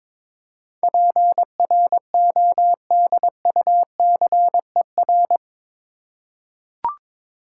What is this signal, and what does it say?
Morse code, heard as a keyed tone, about 22 wpm, 700 Hz. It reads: PRODUCER
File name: practice / 22wpm / Eight Letter Words - 22wpm